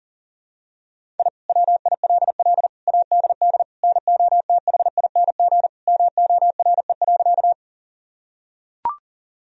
Morse code, heard as a keyed tone, about 40 wpm, 700 Hz. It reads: I WILL ADD NOTHING MORE.
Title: Sentences from Top 1000 Words - Review Part 1 - 40wpm